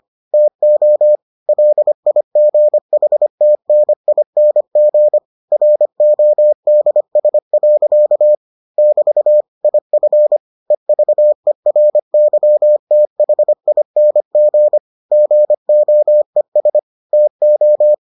Morse code, heard as a keyed tone, about 25 wpm, 600 Hz. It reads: TO LIGHTNING RODS. <BT> IF EVERYTHING GOES TO